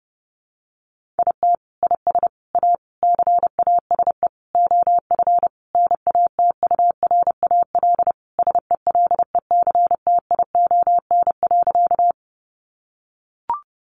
Morse code, heard as a keyed tone, about 30 wpm, 700 Hz. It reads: IT IS A CASE OF NATURAL SELECTION.
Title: Sentences from Top 1400 Words 30wpm